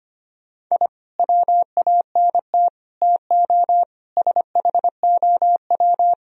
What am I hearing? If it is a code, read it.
Morse code, heard as a keyed tone, about 31 wpm, 700 Hz. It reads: I WANT TO SHOW
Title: Speed-Racing - Sets of 5 Words 25wpm